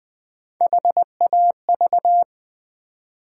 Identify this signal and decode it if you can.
Morse code, heard as a keyed tone, about 20 wpm, 700 Hz. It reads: HAV
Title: Build A Word - 1 Letter At A Time - 3 to 5-Letter Words - 20wpm